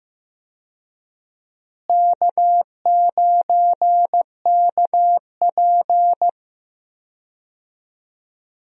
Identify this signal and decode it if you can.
Morse code, heard as a keyed tone, about 15 wpm, 700 Hz. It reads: K9KP
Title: USA Call Signs - 4 Characters - Repeated 2x - 15wpm